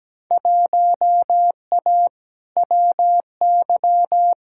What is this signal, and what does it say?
Morse code, heard as a keyed tone, about 17 wpm, 700 Hz. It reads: 1A WY
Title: Field Day Exchanges Encore - 17wpm